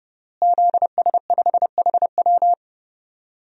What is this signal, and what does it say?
Morse code, heard as a keyed tone, about 32 wpm, 700 Hz. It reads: ZS5HW
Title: DXCC-100 Call Signs - 5 Characters - Repeated 3x - 30wpm